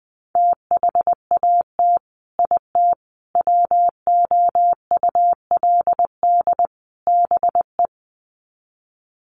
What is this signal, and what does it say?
Morse code, heard as a keyed tone, about 20 wpm, 700 Hz. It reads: THAT IT WOULD BE
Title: Sets of 4 Words 20wpm